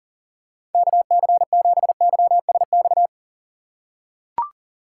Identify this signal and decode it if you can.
Morse code, heard as a keyed tone, about 37 wpm, 700 Hz. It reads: KC7YSX
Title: USA Call Signs - 2x Character Spacing - 40wpm